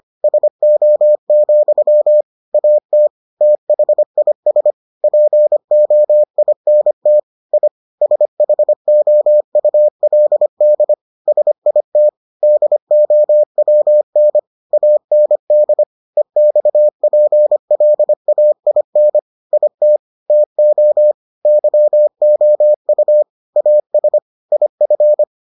Morse code, heard as a keyed tone, about 25 wpm, 600 Hz. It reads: SO, AT THIS POINT I SHOULD SIT DOWN AND EXPLAIN IT TO YOU AS IF